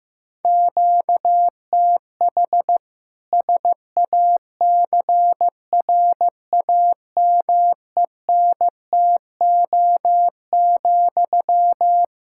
Morse code, heard as a keyed tone, about 15 wpm, 700 Hz. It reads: QTH SACRAMENTO,
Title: Practice QSO - Exchange 1 - Beginning - 15wpm